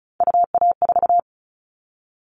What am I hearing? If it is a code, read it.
Morse code, heard as a keyed tone, about 36 wpm, 700 Hz. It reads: UA4
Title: Callsign Prefixes for Top 50 DXCC - 35wpm